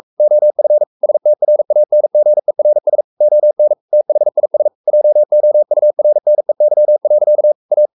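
Morse code, heard as a keyed tone, about 44 wpm, 600 Hz. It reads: OF STRANGERS ON THIS JOURNEY. A
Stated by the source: Morse code at 43 wpm sending Ken Follett's novel Never in English, from CW med Gnister.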